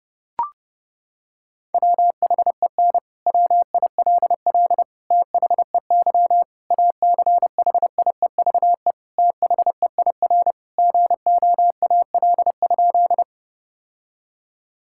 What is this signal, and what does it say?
Morse code, heard as a keyed tone, about 32 wpm, 700 Hz. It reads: WHEN WILL THEY ACHIEVE THEIR GOAL?
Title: Sentences from Top 1000 Words - Review Part 6 - 30wpm